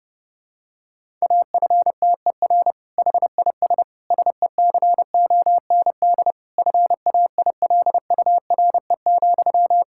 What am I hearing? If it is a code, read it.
Morse code, heard as a keyed tone, about 28 wpm, 700 Hz. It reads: AFTER HIS SECOND FAILURE,
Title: Sentences from Top 1000 Words - Review Part 6 - 30wpm